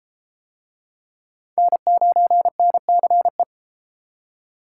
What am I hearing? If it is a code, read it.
Morse code, heard as a keyed tone, about 30 wpm, 700 Hz. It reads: N9NCE